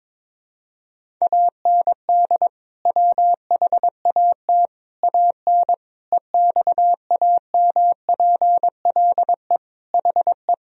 Morse code, heard as a keyed tone, about 22 wpm, 700 Hz. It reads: AND WHAT AN EXAMPLE HE
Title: Sentences from Top 1000 Words - Review Part 1 - 22wpm